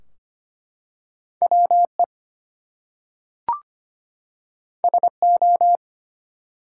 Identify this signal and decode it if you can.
Morse code, heard as a keyed tone, about 25 wpm, 700 Hz. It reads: WE SO
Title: Two Letter Words - 25wpm